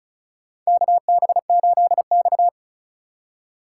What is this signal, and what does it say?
Morse code, heard as a keyed tone, about 36 wpm, 700 Hz. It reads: KB8X
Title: USA Call Signs - 4 Characters - 3x Character Spacing - 35wpm